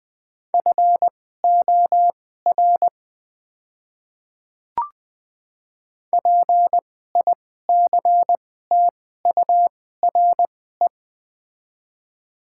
Morse code, heard as a keyed tone, about 21 wpm, spaced as for 15 wpm, 690 Hz. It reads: FOR PICTURE
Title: 2x Character Spacing - Top 500 Words - 20wpm